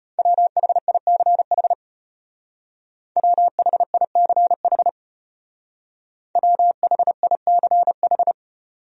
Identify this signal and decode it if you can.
Morse code, heard as a keyed tone, about 33 wpm, 700 Hz. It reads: WHICH WHICH WHICH